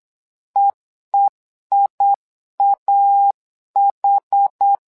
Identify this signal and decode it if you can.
Morse code, heard as a keyed tone, about 8 wpm, 800 Hz. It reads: EEIAH
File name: LEZIONI / 2 UHDN / 2 UH 8wpm 2129